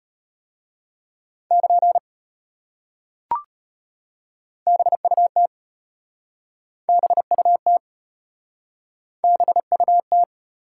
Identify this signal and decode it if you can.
Morse code, heard as a keyed tone, about 32 wpm, 700 Hz. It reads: <KN> BUT BUT BUT